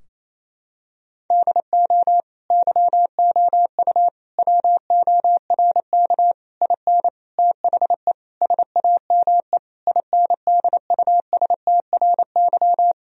Morse code, heard as a keyed tone, about 28 wpm, 700 Hz. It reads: DO YOU WORK IN THE SAME INDUSTRY
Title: Sentences from Top 1000 Words - Review Part 4 - 28wpm